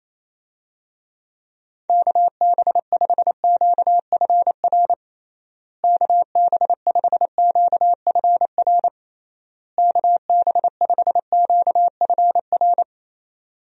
Morse code, heard as a keyed tone, about 28 wpm, 700 Hz. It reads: KB5QFR KB5QFR KB5QFR
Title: USA Call Signs - 6 Characters - Repeated 3x - 28wpm